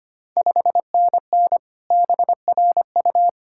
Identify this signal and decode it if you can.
Morse code, heard as a keyed tone, about 25 wpm, 700 Hz. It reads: 5NN BRU
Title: Ontario QSO Party Exchange 25wpm